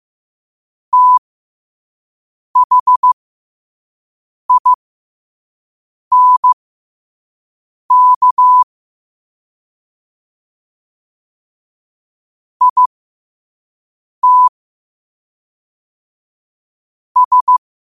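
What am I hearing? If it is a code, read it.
Morse code, heard as a keyed tone, about 15 wpm, spaced as for 5 wpm, 1000 Hz. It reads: THINK IT S
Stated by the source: Morse quote-of-the-day practice at 5 wpm.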